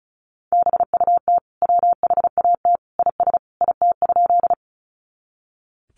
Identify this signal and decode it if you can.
Morse code, heard as a keyed tone, about 34 wpm, 700 Hz. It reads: BUT WHAT IS IT?
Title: Sentences from Top 100 Words 35wpm